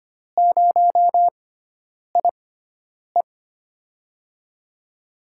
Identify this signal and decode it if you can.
Morse code, heard as a keyed tone, about 25 wpm, spaced as for 9 wpm, 700 Hz. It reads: 0IE
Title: USA Call Signs - 4 Characters - 6x Character Spacing - 25wpm